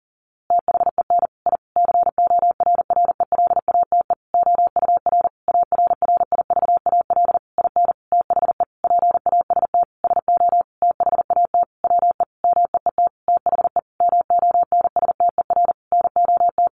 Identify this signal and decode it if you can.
Morse code, heard as a keyed tone, about 40 wpm, 700 Hz. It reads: THEN I CORRELATE OUR ARRIVAL IN THE PAST SO THAT WE MEET THE MONSTER NOT
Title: A Sound Of Thunder 40wpm